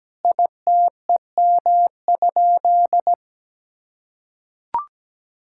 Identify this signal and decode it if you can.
Morse code, heard as a keyed tone, about 17 wpm, 700 Hz. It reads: ITEM?
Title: Sentences from Top 1000 Words - Review Part 6 - 17wpm